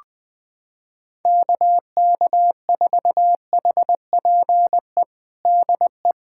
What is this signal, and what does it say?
Morse code, heard as a keyed tone, about 20 wpm, 700 Hz. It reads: KK4HPE DE
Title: USA Callsign DE Callsign - 6 Characters - 20wpm